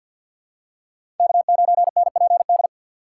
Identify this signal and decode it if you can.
Morse code, heard as a keyed tone, about 46 wpm, 690 Hz. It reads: K9NPD
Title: USA Call Signs - 2x Character Spacing - 50wpm